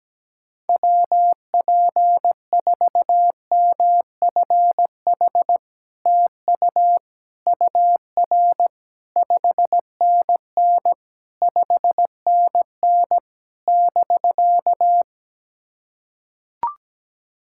Morse code, heard as a keyed tone, about 17 wpm, 700 Hz. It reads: WP4MFH TU UR 5NN 5NN <BK>
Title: SOTA QSO Practice 17wpm